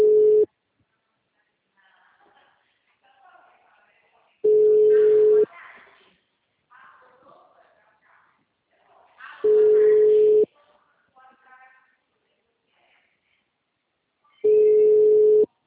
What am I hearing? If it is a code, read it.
Ringback tone (Europe, 425 Hz), with speech.